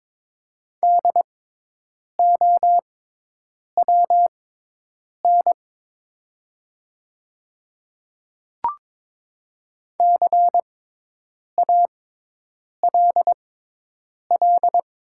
Morse code, heard as a keyed tone, about 23 wpm, spaced as for 8 wpm, 700 Hz. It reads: DOWN CALL
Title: 6x Character Spacing - Top 300 Words - 22wpm